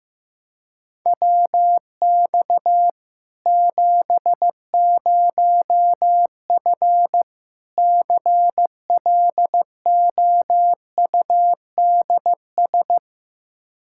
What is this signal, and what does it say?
Morse code, heard as a keyed tone, about 15 wpm, 700 Hz. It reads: WX 70F CLOUDS
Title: Practice QSO - Exchange 2 - Info - 15wpm